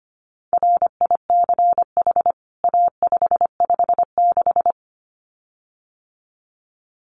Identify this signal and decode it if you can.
Morse code, heard as a keyed tone, about 25 wpm, 700 Hz. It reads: RICH A556
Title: CWT Exchange 25wpm